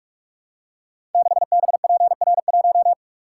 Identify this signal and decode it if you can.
Morse code, heard as a keyed tone, about 44 wpm, 700 Hz. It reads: 6BPR1
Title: Code Groups - 5 Letter:Numbers - 45wpm